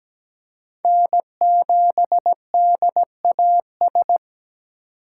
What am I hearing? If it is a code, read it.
Morse code, heard as a keyed tone, about 17 wpm, 700 Hz. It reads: N7DAS
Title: USA Call Signs - 5 Characters - 8x Character Spacing - 17wpm